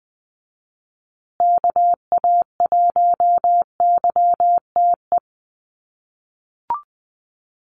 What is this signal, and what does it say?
Morse code, heard as a keyed tone, about 20 wpm, 700 Hz. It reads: KA1YTE
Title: USA Call Signs Repeated 3x 20wpm